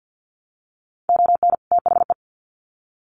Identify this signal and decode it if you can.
Morse code, heard as a keyed tone, about 50 wpm, 700 Hz. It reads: ON THE
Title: Sets of 2 Words 50wpm